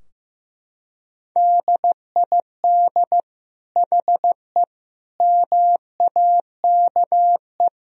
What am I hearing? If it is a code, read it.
Morse code, heard as a keyed tone, about 16 wpm, 700 Hz. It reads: DID HE MAKE
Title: Sentences from Top 1000 Words - Review Part 2 - 15wpm